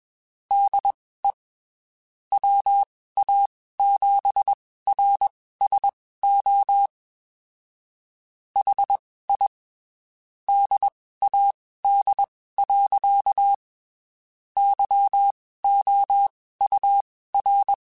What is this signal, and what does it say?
Morse code, heard as a keyed tone, about 21 wpm, spaced as for 15 wpm, 800 Hz. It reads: DE WA7RSO HI DAD. YOUR
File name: (spaced) / QSOs Spaced-mp3s / WD0FBT @20 sp